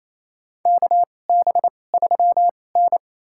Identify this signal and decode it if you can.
Morse code, heard as a keyed tone, about 28 wpm, spaced as for 20 wpm, 700 Hz. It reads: KB3N
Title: USA Call Signs - 4 Characters - 2x Character Spacing - 28wpm